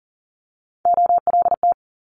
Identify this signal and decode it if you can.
Morse code, heard as a keyed tone, about 42 wpm, 700 Hz. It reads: OLT
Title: Multi-letter Phonemes Quick Fire 40wpm